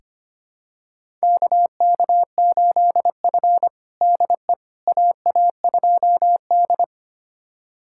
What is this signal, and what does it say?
Morse code, heard as a keyed tone, about 25 wpm, 700 Hz. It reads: KK8F DE AA2D
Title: USA Callsign DE Callsign - 4 Characters - 25wpm